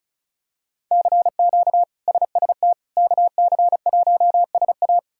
Morse code, heard as a keyed tone, about 34 wpm, 700 Hz. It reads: CQ SST KC1SA
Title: K1USN SST QSO Practice - 35wpm